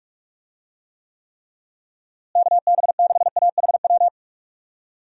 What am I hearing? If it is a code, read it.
Morse code, heard as a keyed tone, about 42 wpm, 700 Hz. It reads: KB6AHW
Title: USA Call Signs - 6 Characters - Repeated 3x - 45wpm